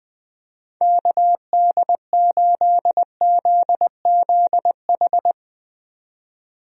Morse code, heard as a keyed tone, about 21 wpm, 700 Hz. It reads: KD8ZZH
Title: USA Call Signs - 6 Characters - 20wpm